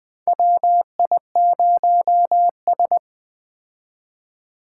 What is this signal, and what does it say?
Morse code, heard as a keyed tone, about 20 wpm, 700 Hz. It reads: WI0S
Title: 2023 POTA Activator Call Signs - 20wpm